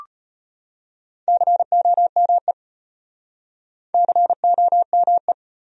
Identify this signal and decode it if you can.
Morse code, heard as a keyed tone, about 33 wpm, 700 Hz. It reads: COME COME